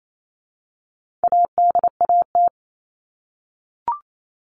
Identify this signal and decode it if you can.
Morse code, heard as a keyed tone, about 28 wpm, 700 Hz. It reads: ADAT
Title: All Cut Numbers - 4-Digit 28wpm